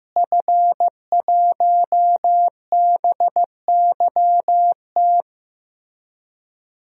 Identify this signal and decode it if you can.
Morse code, heard as a keyed tone, about 15 wpm, 700 Hz. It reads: F1BYT
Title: DXCC-100 Call Signs - 5 Characters - Repeated 3x - 15wpm